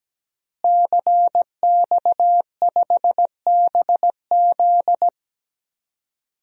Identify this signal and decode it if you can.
Morse code, heard as a keyed tone, about 17 wpm, 700 Hz. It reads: CX5BZ
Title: DXCC-100 Call Signs - 5 Characters - Repeated 2x - 17wpm